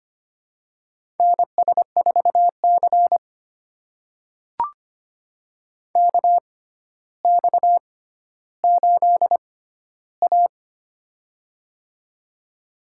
Morse code, heard as a keyed tone, about 25 wpm, spaced as for 13 wpm, 700 Hz. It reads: NS4C KX8A
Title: USA Call Signs - 4 Characters - 6x Character Spacing - 25wpm